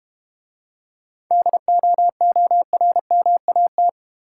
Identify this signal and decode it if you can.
Morse code, heard as a keyed tone, about 30 wpm, 700 Hz. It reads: DOORMAT
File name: Easy Words - Speed-Racing with Hints - 25wpm